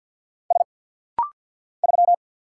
Morse code, heard as a keyed tone, about 48 wpm, 700 Hz. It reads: S 3
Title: Single Letter-Number - Rapid-Fire 50wpm